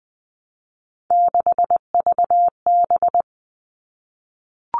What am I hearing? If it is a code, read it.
Morse code, heard as a keyed tone, about 20 wpm, 700 Hz. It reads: 6VB
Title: All Cut Numbers - 3-Digit 20wpm